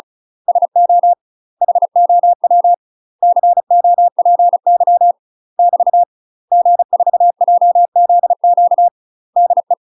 Morse code, heard as a keyed tone, about 36 wpm, 700 Hz. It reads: SO HOW COPY <BT> G4JZQ DE
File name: UK MASTER QSOs TRAINING / 35WPM / UK MASTER QSOs 0018